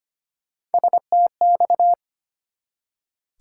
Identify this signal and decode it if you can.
Morse code, heard as a keyed tone, about 25 wpm, 700 Hz. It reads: STX
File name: practice / 25wpm / ARRL Sections v2 - 25wpm